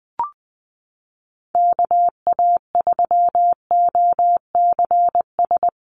Morse code, heard as a keyed tone, about 20 wpm, 700 Hz. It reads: KA3OCS
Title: Call Signs - 20wpm